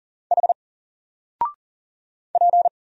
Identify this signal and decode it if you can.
Morse code, heard as a keyed tone, about 40 wpm, 700 Hz. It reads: H P
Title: Single Character (Letter:Number:Punctuation:Prosign) Alt 40wpm